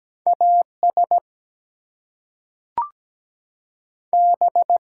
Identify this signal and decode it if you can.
Morse code, heard as a keyed tone, about 17 wpm, 700 Hz. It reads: AS B
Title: Two Letter Words - 17wpm